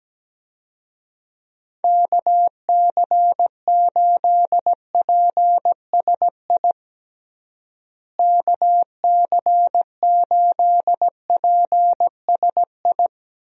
Morse code, heard as a keyed tone, about 17 wpm, 700 Hz. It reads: KC8PSI KC8PSI
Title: USA Call Signs - 6 Characters - Repeated 3x - 17wpm